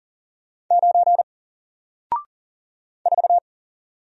Morse code, heard as a keyed tone, about 40 wpm, 700 Hz. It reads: 9 4
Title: Single Letter-Number Alt 40wpm